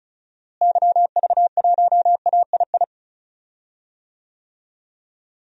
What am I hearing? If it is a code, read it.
Morse code, heard as a keyed tone, about 34 wpm, 700 Hz. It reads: YV1AII
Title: DXCC-100 Call Signs - 6 Characters - Repeated 2x - 35wpm